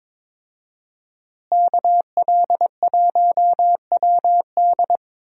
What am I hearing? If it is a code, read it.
Morse code, heard as a keyed tone, about 22 wpm, 700 Hz. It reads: KL1WD
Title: USA Call Signs - 5 Characters - Repeated 2x - 22wpm